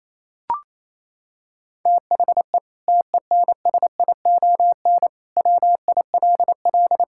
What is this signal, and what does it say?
Morse code, heard as a keyed tone, about 28 wpm, 700 Hz. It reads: THE TENSION WILL
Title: Sentences from Top 1600 Words 28wpm